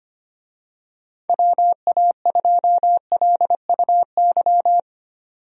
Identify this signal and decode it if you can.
Morse code, heard as a keyed tone, about 25 wpm, 700 Hz. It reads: WA2LUY
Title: USA Call Signs - 6 Characters - 25wpm